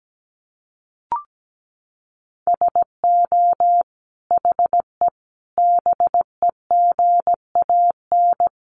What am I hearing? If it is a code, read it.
Morse code, heard as a keyed tone, about 17 wpm, 700 Hz. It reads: SO HE BEGAN
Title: Sentences from Top 300 Words 17wpm